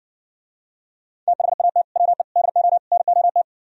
Morse code, heard as a keyed tone, about 60 wpm, 700 Hz. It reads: THAT WE DO NOT